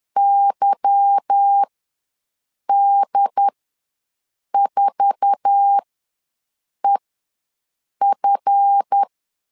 Morse code, heard as a keyed tone, about 10 wpm, spaced as for 6 wpm, 800 Hz. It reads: YD4EF
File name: LEZIONI / 10 SIGNS / 10B INT ws 10wpm 4134